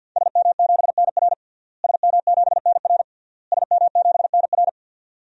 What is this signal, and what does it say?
Morse code, heard as a keyed tone, about 47 wpm, 700 Hz. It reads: SM7NR SM7NR SM7NR
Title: DXCC-100 Call Signs - 5 Characters - Repeated 3x - 50wpm